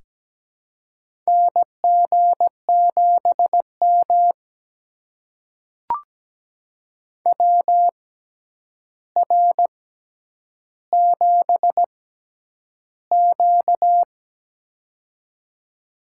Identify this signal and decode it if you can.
Morse code, heard as a keyed tone, about 17 wpm, spaced as for 9 wpm, 700 Hz. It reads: NG7M WR7Q
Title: USA Call Signs - 4 Characters - 6x Character Spacing - 17wpm